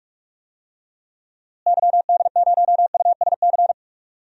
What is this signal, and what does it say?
Morse code, heard as a keyed tone, about 47 wpm, 700 Hz. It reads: YD0USC
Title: DXCC-100 Call Signs - 6 Characters - Repeated 2x - 45wpm